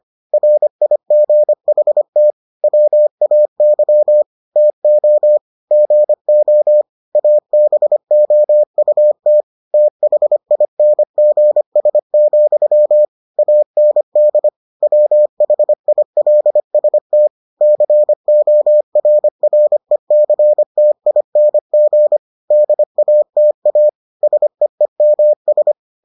Morse code, heard as a keyed tone, about 25 wpm, 600 Hz. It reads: RIGHT WAY TO GO ABOUT THINGS, AND WHILST CORRECTING DATA SEEMS